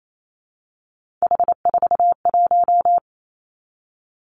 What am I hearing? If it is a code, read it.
Morse code, heard as a keyed tone, about 28 wpm, 690 Hz. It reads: H41
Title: Three Challenging Letter-Number Combinations 28wpm